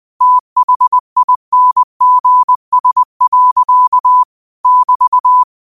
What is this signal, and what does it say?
Morse code, heard as a keyed tone, about 20 wpm, 1000 Hz. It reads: THINGS. <BT>
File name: qotdfiles / 20wpm